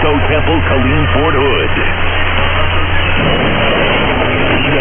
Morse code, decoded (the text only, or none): none